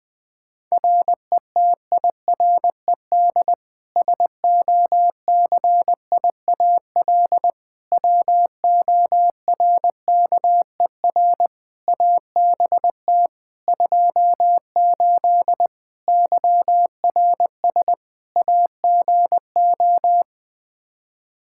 Morse code, heard as a keyed tone, about 20 wpm, 700 Hz. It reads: RETIRED SOCIAL WORKER ABT 28 YRS AGO
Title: K1USN SST QSO Practice 20wpm